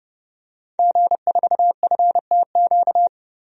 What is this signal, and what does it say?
Morse code, heard as a keyed tone, about 32 wpm, 700 Hz. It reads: G4FTQ
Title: DXCC-100 Call Signs - 5 Characters - Repeated 2x - 30wpm